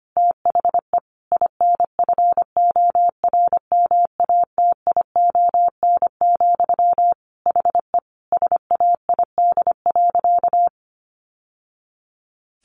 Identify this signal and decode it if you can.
Morse code, heard as a keyed tone, about 25 wpm, 700 Hz. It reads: THE INFORMATION, HE SAID.
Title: Sentences from Top 600 Words 25wpm